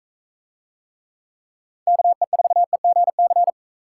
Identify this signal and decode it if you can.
Morse code, heard as a keyed tone, about 41 wpm, 700 Hz. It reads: KE4EGC